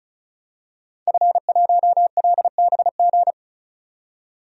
Morse code, heard as a keyed tone, about 36 wpm, 700 Hz. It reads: F1LBG
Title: DXCC-100 Call Signs - 5 Characters - 35wpm